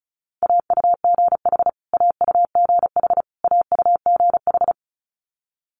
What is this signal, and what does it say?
Morse code, heard as a keyed tone, about 36 wpm, 700 Hz. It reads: AUGH AUGH AUGH
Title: Multi-letter Phonemes Repeated 3x 35wpm